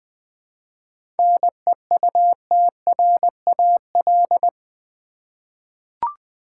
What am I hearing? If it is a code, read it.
Morse code, heard as a keyed tone, about 21 wpm, 700 Hz. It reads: NEUTRAL
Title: 7 Letter Words - First 3 Letters Given - 20wpm